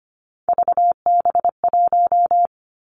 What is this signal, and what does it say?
Morse code, heard as a keyed tone, about 25 wpm, 700 Hz. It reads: VB1
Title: Three Challenging Letter-Number Combinations 25wpm